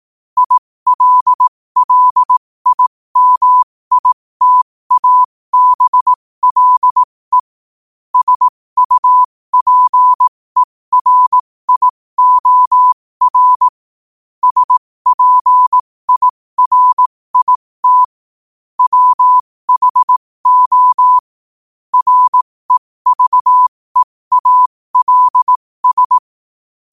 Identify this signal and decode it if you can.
Morse code, heard as a keyed tone, about 18 wpm, spaced as for 15 wpm, 1000 Hz. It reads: ILLIMITABLE SUPERIOR SPIRIT WHO REVEALS
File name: qotdfiles / 15wpm